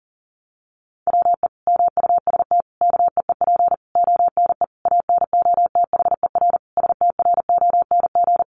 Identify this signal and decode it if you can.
Morse code, heard as a keyed tone, about 40 wpm, 700 Hz. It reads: WE MUST KEEP ONE ANOTHER STRONG
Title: Sentences from Top 600 Words 40wpm